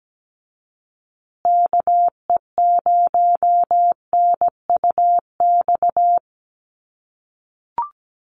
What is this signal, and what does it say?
Morse code, heard as a keyed tone, about 17 wpm, 700 Hz. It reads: KE0NUX
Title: USA Call Signs Repeated 3x 17wpm